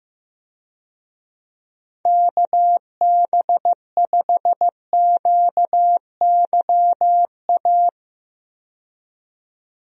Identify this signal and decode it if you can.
Morse code, heard as a keyed tone, about 15 wpm, 700 Hz. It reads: KB5QYA
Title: USA Call Signs - 6 Characters - 15wpm